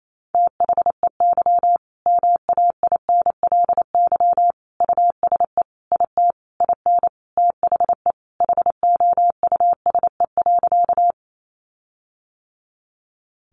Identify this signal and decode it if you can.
Morse code, heard as a keyed tone, about 28 wpm, 700 Hz. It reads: THEY MAINLY USE IT IN THE HOUSE.
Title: Sentences from Top 1400 Words 28wpm